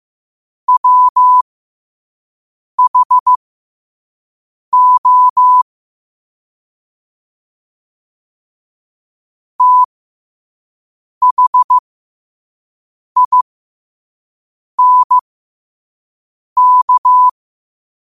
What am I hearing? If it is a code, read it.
Morse code, heard as a keyed tone, about 15 wpm, spaced as for 5 wpm, 1000 Hz. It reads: WHO THINK
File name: qotdfiles / 5wpm